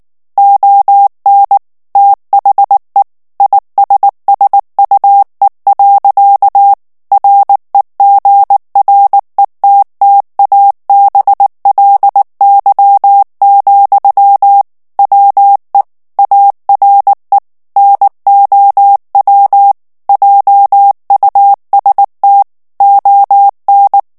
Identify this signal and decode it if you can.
Morse code, heard as a keyed tone, about 19 wpm, 800 Hz. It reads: ON THE ISSUE. REGRETTABLY, WE ARE NOW JUST ON
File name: morse mp3 / ANGLAIS texte clair 2 vitesse 18 mots minute